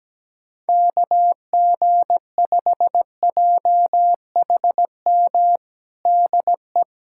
Morse code, heard as a keyed tone, about 17 wpm, 700 Hz. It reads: KG5JHM DE
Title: USA Callsign DE Callsign - 6 Characters - 17wpm